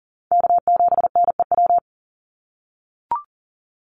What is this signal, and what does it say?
Morse code, heard as a keyed tone, about 40 wpm, 700 Hz. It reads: K7NEW
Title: NA SOTA Chaser Callsigns 40wpm